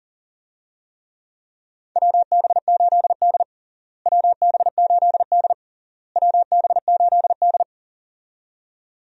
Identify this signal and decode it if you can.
Morse code, heard as a keyed tone, about 37 wpm, 700 Hz. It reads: WB8D WB8D WB8D
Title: USA Call Signs - 4 Characters - Repeated 3x - 40wpm